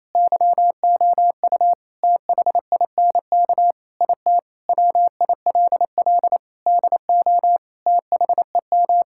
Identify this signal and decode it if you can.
Morse code, heard as a keyed tone, about 28 wpm, 700 Hz. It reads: YOU THINK IT WILL DO THEM
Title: Sentences from Top 1000 Words - Review Part 6 - 28wpm